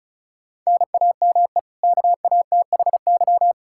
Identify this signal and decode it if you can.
Morse code, heard as a keyed tone, about 34 wpm, 700 Hz. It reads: NAME KATHY
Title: Practice QSO - Exchange 1 - Beginning - 35wpm